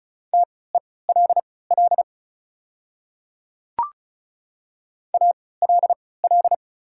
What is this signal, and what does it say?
Morse code, heard as a keyed tone, about 34 wpm, spaced as for 20 wpm, 690 Hz. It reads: TELL ALL
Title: 3x Character Spacing - Top 500 Words - 35wpm